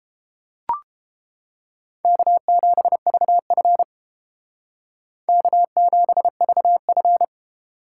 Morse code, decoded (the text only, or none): K7VF K7VF